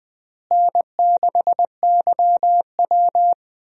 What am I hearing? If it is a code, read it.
Morse code, heard as a keyed tone, about 20 wpm, 700 Hz. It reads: N6YW
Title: USA Call Signs - 4 Characters - Repeated 2x - 20wpm